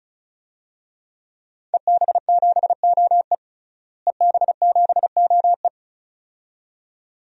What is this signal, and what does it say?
Morse code, heard as a keyed tone, about 36 wpm, 700 Hz. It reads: EB7OE EB7OE
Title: DXCC-100 Call Signs - 5 Characters - Repeated 2x - 35wpm